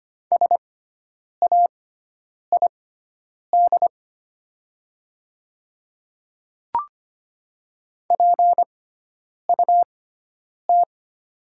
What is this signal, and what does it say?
Morse code, heard as a keyed tone, about 25 wpm, spaced as for 9 wpm, 700 Hz. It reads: SAID PUT
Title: 6x Character Spacing - Top 200 Words - 25wpm